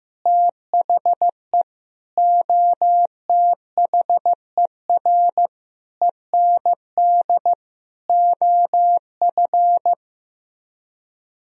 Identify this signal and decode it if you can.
Morse code, heard as a keyed tone, about 15 wpm, 700 Hz. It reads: THE OTHER END OF
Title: Speed-Racing - Sets of 5 Words 15wpm